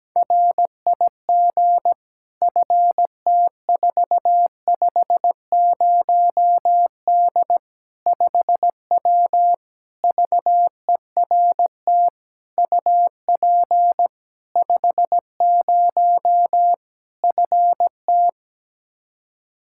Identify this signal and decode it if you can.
Morse code, heard as a keyed tone, about 17 wpm, 700 Hz. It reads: RIG FT450D 5W VERT UP 50 FT
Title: Practice QSO - Exchange 2 - Info - 17wpm